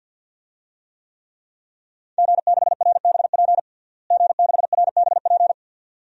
Morse code, heard as a keyed tone, about 51 wpm, 700 Hz. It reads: G6RBP G6RBP
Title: DXCC-100 Call Signs - 5 Characters - Repeated 2x - 50wpm